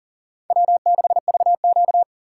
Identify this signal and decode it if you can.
Morse code, heard as a keyed tone, about 40 wpm, 700 Hz. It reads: W6VQ
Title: USA Call Signs - 4 Characters - 6x Character Spacing - 40wpm